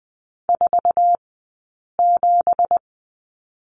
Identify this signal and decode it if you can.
Morse code, heard as a keyed tone, about 20 wpm, spaced as for 8 wpm, 700 Hz. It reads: 47
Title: 165 Lesson 27d - Number 7 - Two Characters 20wpm